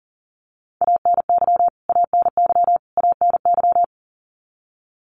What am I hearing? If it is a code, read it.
Morse code, heard as a keyed tone, about 40 wpm, 700 Hz. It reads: ANY ANY ANY
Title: Top 200 Words Repeated 40wpm